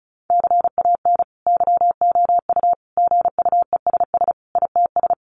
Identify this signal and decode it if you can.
Morse code, heard as a keyed tone, about 34 wpm, 700 Hz. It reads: CAN YOU GUESS ITS
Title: Sentences from Top 1500 Words 35wpm